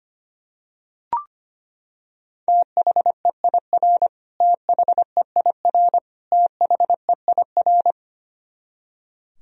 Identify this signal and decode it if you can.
Morse code, heard as a keyed tone, about 25 wpm, 700 Hz. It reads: THEIR THEIR THEIR
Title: Top 100 Words Repeated 25wpm